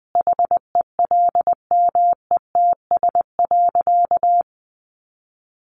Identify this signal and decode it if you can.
Morse code, heard as a keyed tone, about 20 wpm, 700 Hz. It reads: HELMETS.
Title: A Sound Of Thunder 20wpm